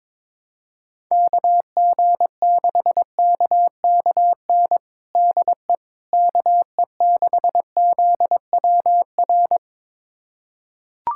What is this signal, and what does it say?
Morse code, heard as a keyed tone, about 22 wpm, 700 Hz. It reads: KG6KKN DE KE6ZWR
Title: USA Callsign DE Callsign - 6 Characters - 22wpm